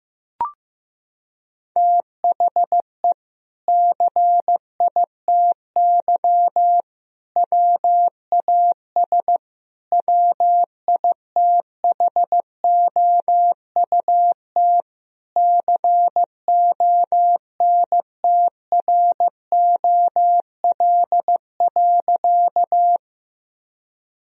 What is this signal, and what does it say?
Morse code, heard as a keyed tone, about 14 wpm, 700 Hz. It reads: THE CITY WAS WITHOUT CONTROL.
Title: Sentences from Top 1000 Words - Review Part 3 - 15wpm